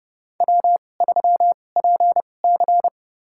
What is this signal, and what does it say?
Morse code, heard as a keyed tone, about 32 wpm, spaced as for 22 wpm, 700 Hz. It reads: W3PC
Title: USA Call Signs - 2x Character Spacing - 30wpm